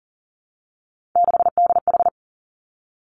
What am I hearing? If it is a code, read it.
Morse code, heard as a keyed tone, about 40 wpm, 700 Hz. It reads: 6DH
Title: Three Challenging Letter-Number Combinations 40wpm